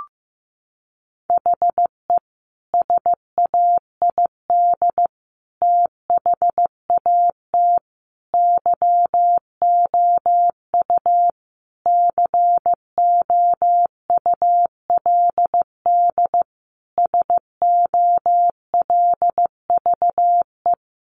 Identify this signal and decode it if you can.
Morse code, heard as a keyed tone, about 15 wpm, 700 Hz. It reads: HE SAID THAT YOU COULD SOLVE
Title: Sentences from Top 1000 Words 15wpm